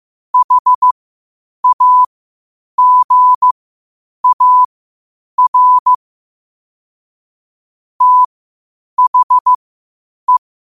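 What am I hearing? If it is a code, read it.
Morse code, heard as a keyed tone, about 15 wpm, spaced as for 8 wpm, 1000 Hz. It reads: HAGAR THE
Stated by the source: Morse code quotes at 8 WPM.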